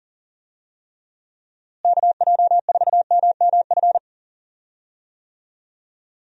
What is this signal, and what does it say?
Morse code, heard as a keyed tone, about 40 wpm, 700 Hz. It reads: KJ4MMF
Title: USA Call Signs - 6 Characters - Repeated 3x - 40wpm